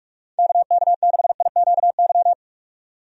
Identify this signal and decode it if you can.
Morse code, heard as a keyed tone, about 46 wpm, 700 Hz. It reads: KK6IQY
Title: USA Call Signs - 6 Characters - 45wpm